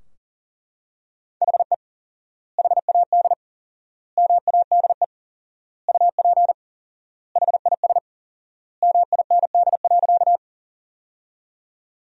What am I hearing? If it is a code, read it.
Morse code, heard as a keyed tone, about 40 wpm, 700 Hz. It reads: HE HAD MADE UP HIS MIND.
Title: 4x Word-Spacing - Sentences from Top 500 Words - 40wpm